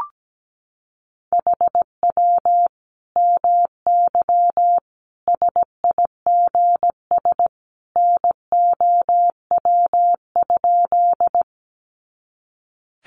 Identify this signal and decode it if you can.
Morse code, heard as a keyed tone, about 17 wpm, 700 Hz. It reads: HW MY SIGS NOW?
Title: Common QSO Phrases 17wpm